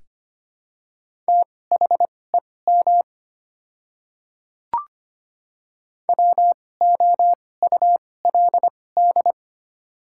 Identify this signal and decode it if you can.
Morse code, heard as a keyed tone, about 25 wpm, spaced as for 18 wpm, 700 Hz. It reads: THEM WOULD
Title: 2x Character Spacing - Top 200 Words - 25wpm